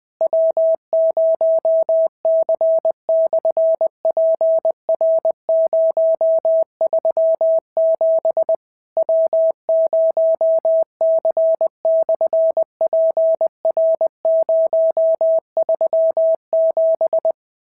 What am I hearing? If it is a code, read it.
Morse code, heard as a keyed tone, about 20 wpm, 690 Hz. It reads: W0C/PR037 W0C/PR037
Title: SOTA QSO Practice 20wpm